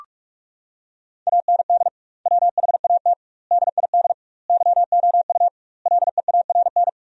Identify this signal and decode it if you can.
Morse code, heard as a keyed tone, about 41 wpm, 700 Hz. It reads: AND WHAT DID YOU LEARN
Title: 2x Word-Spacing - Sentences from Top 300 Words - 45wpm